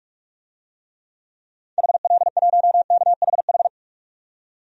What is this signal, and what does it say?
Morse code, heard as a keyed tone, about 44 wpm, 700 Hz. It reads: HL1KHH
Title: DXCC-100 Call Signs - 6 Characters - 45wpm